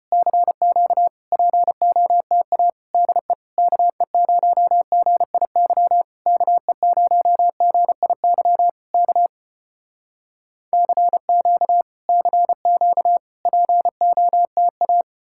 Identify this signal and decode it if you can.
Morse code, heard as a keyed tone, about 35 wpm, 700 Hz. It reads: CQ POTA DE KE0GIY KE0GIY K CQ CQ POTA